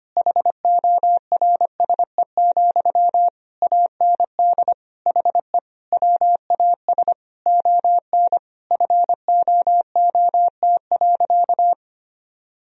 Morse code, heard as a keyed tone, about 25 wpm, 700 Hz. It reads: HORSE, AND HE WAS ON FOOT.
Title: Sentences from Top 1000 Words - Review Part 3 - 25wpm